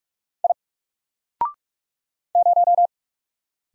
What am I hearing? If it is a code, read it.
Morse code, heard as a keyed tone, about 44 wpm, 700 Hz. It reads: I 0
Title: Single Character (Letter:Number:Punctuation:Prosign) Alt 45wpm